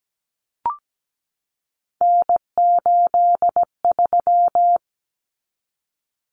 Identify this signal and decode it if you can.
Morse code, heard as a keyed tone, about 17 wpm, 700 Hz. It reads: N83
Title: Common Cut Numbers - 3-Digit 17wpm